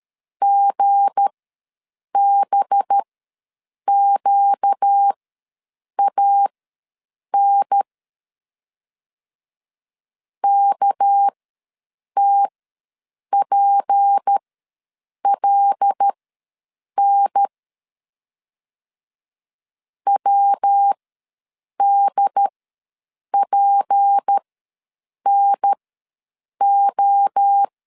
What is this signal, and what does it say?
Morse code, heard as a keyed tone, about 13 wpm, spaced as for 7 wpm, 800 Hz. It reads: GBQAN KTPLN WDPNO
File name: LEZIONI / 7 GF9Z / 7 G ws 12wpm 3140